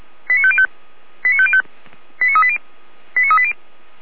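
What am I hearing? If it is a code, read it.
Five-tone selective call (ZVEI), heard as quick runs of five tones, tones of about 70 ms. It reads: 89505 89505 89380 89380
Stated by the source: ZVEI five-tone selective call received on SDR.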